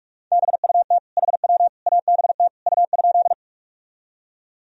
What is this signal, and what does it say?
Morse code, heard as a keyed tone, about 46 wpm, 700 Hz. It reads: BUT HW ABT U?
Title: Sentences from Top 100 Words and Abbreviations 45wpm